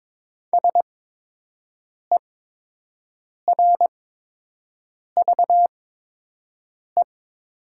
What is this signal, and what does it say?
Morse code, heard as a keyed tone, about 22 wpm, spaced as for 6 wpm, 700 Hz. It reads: SERVE
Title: 8x Character Spacing - Top 500 Words - 22wpm